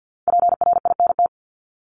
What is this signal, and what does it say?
Morse code, heard as a keyed tone, about 50 wpm, 700 Hz. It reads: PRINT
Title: Top 1251-1500 Words 50wpm